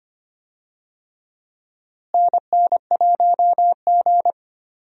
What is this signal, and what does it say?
Morse code, heard as a keyed tone, about 25 wpm, 700 Hz. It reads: NN1G
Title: USA Call Signs - 4 Characters - Repeated 3x - 25wpm